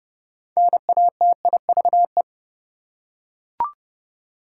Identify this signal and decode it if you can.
Morse code, heard as a keyed tone, about 30 wpm, 700 Hz. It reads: NATIVE
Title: Six Letter Words - 30wpm